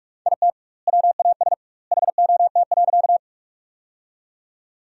Morse code, heard as a keyed tone, about 43 wpm, 700 Hz. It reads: IT WAS HOT.
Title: 2x Word-Spacing - Sentences from Top 500 Words - 45wpm